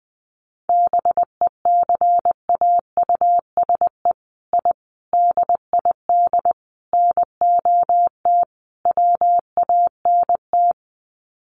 Morse code, heard as a keyed tone, about 20 wpm, 700 Hz. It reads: BECAUSE I DID NOT WANT
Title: Sets of 5 Words Encore2 20wpm